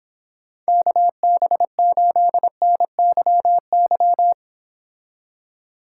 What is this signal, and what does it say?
Morse code, heard as a keyed tone, about 26 wpm, 700 Hz. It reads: KB8NYY